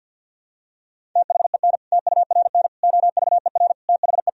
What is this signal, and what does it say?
Morse code, heard as a keyed tone, about 52 wpm, 700 Hz. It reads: THEN TURN OVER THE
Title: Sentences from Top 1000 Words - Review Part 1 - 50wpm